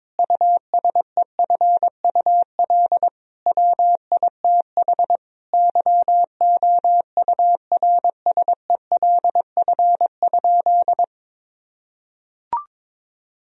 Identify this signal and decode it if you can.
Morse code, heard as a keyed tone, about 22 wpm, 700 Hz. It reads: USEFUL WITH YOURSELF?
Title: Sentences from Top 1000 Words - Review Part 4 - 22wpm